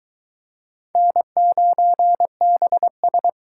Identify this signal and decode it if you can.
Morse code, heard as a keyed tone, about 23 wpm, 700 Hz. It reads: N9BS